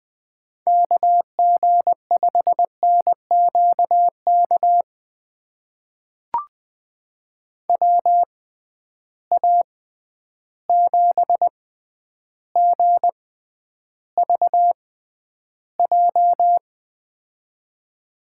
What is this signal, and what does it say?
Morse code, heard as a keyed tone, about 20 wpm, spaced as for 10 wpm, 700 Hz. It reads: KG5NQK WA7GVJ
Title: USA Call Signs - 6 Characters - 6x Character Spacing - 20wpm